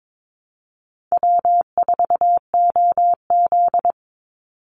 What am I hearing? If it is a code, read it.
Morse code, heard as a keyed tone, about 22 wpm, 700 Hz. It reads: W4OZ
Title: USA Call Signs 22wpm